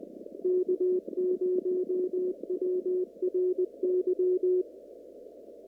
Morse code, heard as a keyed tone, about 19 wpm, 360 Hz. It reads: K0WRY
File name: CW / kt5j wpx cw-op-k5pi